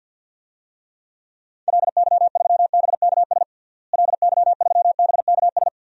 Morse code, heard as a keyed tone, about 46 wpm, 700 Hz. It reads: LY3BKS LY3BKS
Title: DXCC-100 Call Signs - 6 Characters - Repeated 3x - 50wpm